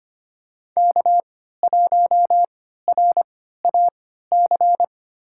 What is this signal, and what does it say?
Morse code, heard as a keyed tone, about 25 wpm, spaced as for 14 wpm, 700 Hz. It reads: K1RAC
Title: USA Call Signs - 5 Characters - 3x Character Spacing - 25wpm